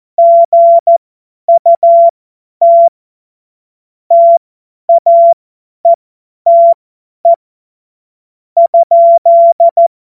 Morse code, heard as a keyed tone, about 14 wpm, spaced as for 10 wpm, 680 Hz. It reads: GUT TAETE ?
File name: Morsekurs / Witze / Witz 10.Oktober 2023 50 F70